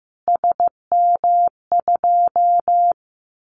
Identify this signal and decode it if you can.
Morse code, heard as a keyed tone, about 15 wpm, 700 Hz. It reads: SM2
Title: Callsign Prefixes for Top 25 DXCC - Repeated 3x - 15wpm